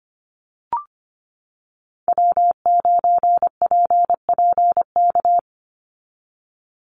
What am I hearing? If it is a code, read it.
Morse code, heard as a keyed tone, about 25 wpm, 700 Hz. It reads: W9PPK
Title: USA Call Signs 25wpm